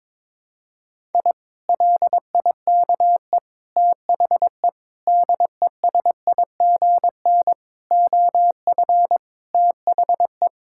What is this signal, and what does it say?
Morse code, heard as a keyed tone, about 22 wpm, 700 Hz. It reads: I LIKE THE DESIGN OF THE
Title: Sentences from Top 1000 Words - Review Part 5 - 22wpm